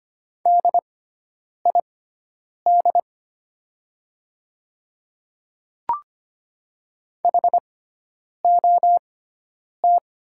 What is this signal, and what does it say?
Morse code, heard as a keyed tone, about 25 wpm, spaced as for 9 wpm, 700 Hz. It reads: DID HOT
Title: 6x Character Spacing - Top 100 Words - 25wpm